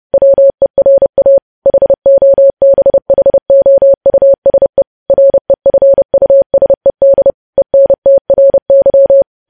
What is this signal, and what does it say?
Morse code, heard as a keyed tone, about 30 wpm, 550 Hz. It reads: WERA HOBHOUSE REFUSED ENTRY
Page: news headlines in morse code at 30 wpm